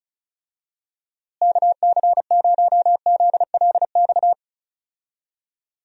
Morse code, heard as a keyed tone, about 34 wpm, 700 Hz. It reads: KC0ZLX
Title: USA Call Signs - 6 Characters - Repeated 2x - 35wpm